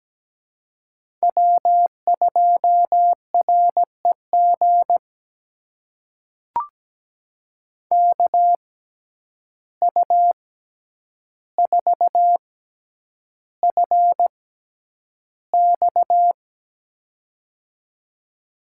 Morse code, heard as a keyed tone, about 17 wpm, spaced as for 9 wpm, 700 Hz. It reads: W2REG KU4FX
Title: USA Call Signs - 5 Characters - 6x Character Spacing - 17wpm